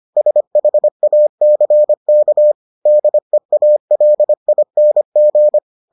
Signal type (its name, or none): Morse code